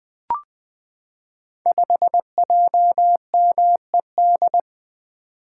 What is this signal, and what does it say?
Morse code, heard as a keyed tone, about 20 wpm, 700 Hz. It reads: 5JMED
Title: Code Groups - 5 Letter:Numbers - 20wpm